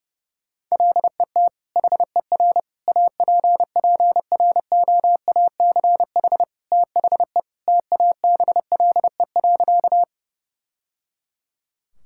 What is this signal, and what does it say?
Morse code, heard as a keyed tone, about 32 wpm, 700 Hz. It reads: LET HER APPROACH THE TABLE.
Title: Sentences from Top 1000 Words - Review Part 6 - 30wpm